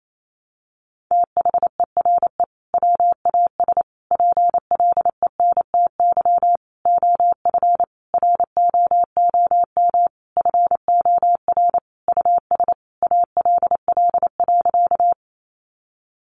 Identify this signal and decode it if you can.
Morse code, heard as a keyed tone, about 28 wpm, 700 Hz. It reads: THERE WAS PLENTY OF ROOM FOR US ALL.
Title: Sentences from Top 1300 Words 28wpm